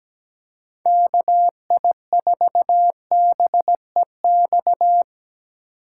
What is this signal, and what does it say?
Morse code, heard as a keyed tone, about 17 wpm, 700 Hz. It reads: KI4BEX
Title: USA Call Signs - 6 Characters - 8x Character Spacing - 17wpm